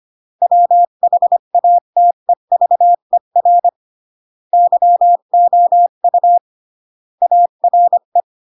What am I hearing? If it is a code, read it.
Morse code, heard as a keyed tone, about 25 wpm, 700 Hz. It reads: WHATEVER YOU ARE
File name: CW / jodorowsky / 82 commandments 25-20